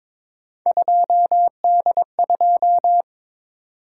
Morse code, heard as a keyed tone, about 22 wpm, 700 Hz. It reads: 2D2
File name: practice / 22wpm / Three Letter-Number Combinations - 22wpm